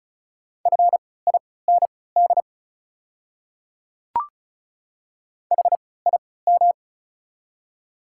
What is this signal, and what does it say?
Morse code, heard as a keyed tone, about 34 wpm, spaced as for 20 wpm, 690 Hz. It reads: FIND HIM
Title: 3x Character Spacing - Top 500 Words - 35wpm